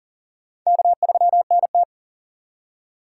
Morse code, heard as a keyed tone, about 40 wpm, 700 Hz. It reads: K3NT
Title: USA Call Signs - 4 Characters - 3x Character Spacing - 40wpm